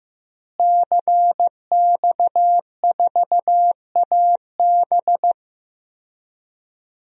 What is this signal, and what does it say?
Morse code, heard as a keyed tone, about 16 wpm, 700 Hz. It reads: CX4AB
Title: DXCC-100 Call Signs - 5 Characters - Repeated 3x - 15wpm